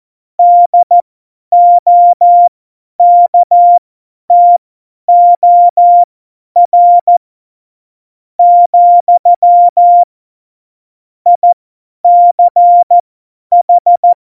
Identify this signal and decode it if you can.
Morse code, heard as a keyed tone, about 14 wpm, spaced as for 9 wpm, 700 Hz. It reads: DOKTOR, ICH
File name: Morsekurs / Witze / Witz 06. Februar 2018, 50 F70